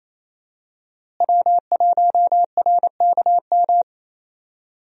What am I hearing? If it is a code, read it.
Morse code, heard as a keyed tone, about 28 wpm, 700 Hz. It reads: W1RKM
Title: USA Call Signs - 5 Characters - Repeated 2x - 28wpm